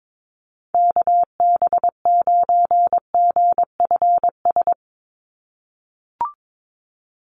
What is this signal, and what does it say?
Morse code, heard as a keyed tone, about 22 wpm, 700 Hz. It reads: KB9GFS
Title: USA Call Signs 22wpm